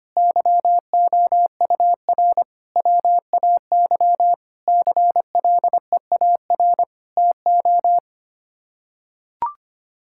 Morse code, heard as a keyed tone, about 25 wpm, 700 Hz. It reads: YOUR WAY CLEAR TO
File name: Sets of 9 Words - 25wpm